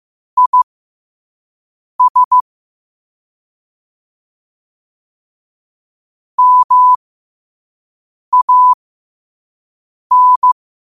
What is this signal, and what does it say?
Morse code, heard as a keyed tone, about 15 wpm, spaced as for 5 wpm, 1000 Hz. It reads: IS MAN
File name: qotdfiles / 5wpm